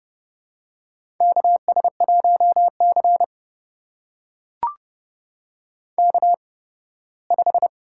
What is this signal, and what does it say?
Morse code, heard as a keyed tone, about 30 wpm, spaced as for 18 wpm, 700 Hz. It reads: KS1C K5
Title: USA Call Signs - 4 Characters - 8x Character Spacing - 30wpm